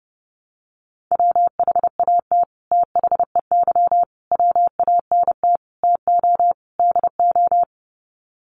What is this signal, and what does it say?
Morse code, heard as a keyed tone, about 30 wpm, 700 Hz. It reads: WHAT THEY WANT TO DO
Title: Sets of 5 Words Encore2 30wpm